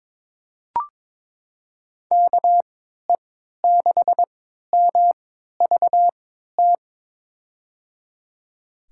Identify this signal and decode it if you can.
Morse code, heard as a keyed tone, about 23 wpm, spaced as for 13 wpm, 700 Hz. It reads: KE6MVT
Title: USA Call Signs - 3x Character Spacing - 22wpm